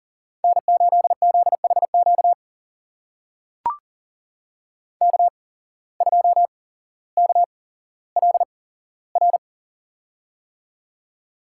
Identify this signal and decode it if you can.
Morse code, heard as a keyed tone, about 40 wpm, spaced as for 17 wpm, 700 Hz. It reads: N8ZHQ K2KLR
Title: USA Call Signs - 5 Characters - 8x Character Spacing - 40wpm